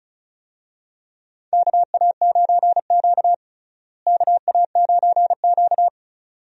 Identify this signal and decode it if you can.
Morse code, heard as a keyed tone, about 34 wpm, 700 Hz. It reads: KA9Q KA9Q
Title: USA Call Signs - 4 Characters - Repeated 2x - 35wpm